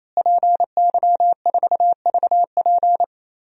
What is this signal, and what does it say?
Morse code, heard as a keyed tone, about 27 wpm, 700 Hz. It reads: PY4VP
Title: DXCC-100 Call Signs - 5 Characters - Repeated 3x - 28wpm